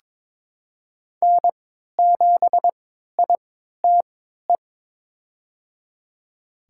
Morse code, heard as a keyed tone, about 22 wpm, spaced as for 13 wpm, 700 Hz. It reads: N7ITE
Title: USA Call Signs - 3x Character Spacing - 22wpm